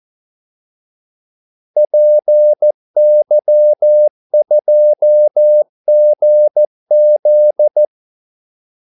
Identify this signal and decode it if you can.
Morse code, heard as a keyed tone, about 14 wpm, 600 Hz. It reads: PY2GZ
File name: Callsigns-14wpm